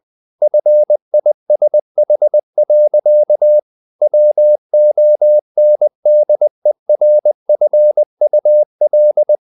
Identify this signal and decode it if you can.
Morse code, heard as a keyed tone, about 20 wpm, 600 Hz. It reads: FISH. WONDERFUL
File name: the old man and the sea / the old man and the sea part 1 20wpm